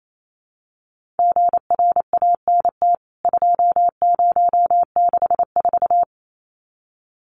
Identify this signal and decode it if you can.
Morse code, heard as a keyed tone, about 28 wpm, 700 Hz. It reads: GRANT 2064
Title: CWT Exchange 28wpm